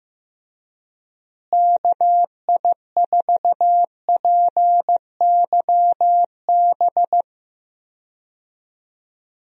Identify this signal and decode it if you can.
Morse code, heard as a keyed tone, about 16 wpm, 700 Hz. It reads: KI4PYB
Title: USA Call Signs - 6 Characters - Repeated 3x - 15wpm